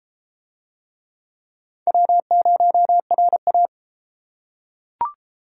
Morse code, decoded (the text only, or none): W0RA